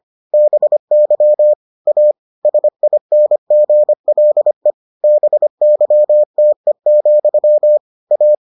Morse code, heard as a keyed tone, about 25 wpm, 600 Hz. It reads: BY A SINGLE BYTE, A